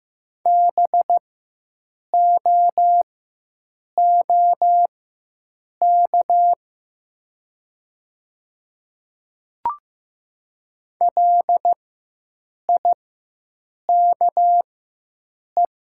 Morse code, heard as a keyed tone, about 15 wpm, spaced as for 7 wpm, 700 Hz. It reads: BOOK LIKE
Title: 4x Character Spacing - Top 300 Words - 15wpm